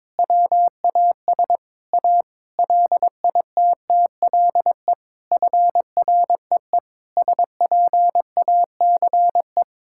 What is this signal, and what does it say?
Morse code, heard as a keyed tone, about 22 wpm, 700 Hz. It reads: WAS A LITTLE FREE SPACE
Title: Sentences from Top 1000 Words - Review Part 5 - 22wpm